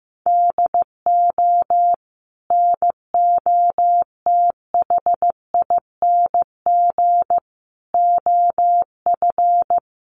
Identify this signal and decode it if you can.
Morse code, heard as a keyed tone, about 15 wpm, 700 Hz. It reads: DO NOTHING OF
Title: Sentences from Top 1300 Words 15wpm